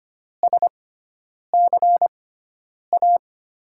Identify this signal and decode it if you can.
Morse code, heard as a keyed tone, about 25 wpm, spaced as for 9 wpm, 700 Hz. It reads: SCA
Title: USA Call Signs - 6 Characters - 6x Character Spacing - 25wpm